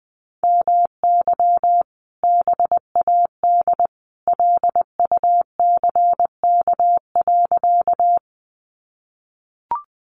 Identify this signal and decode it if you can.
Morse code, heard as a keyed tone, about 20 wpm, 700 Hz. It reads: MY BAD LUCK.
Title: Sentences from Top 1500 Words 20wpm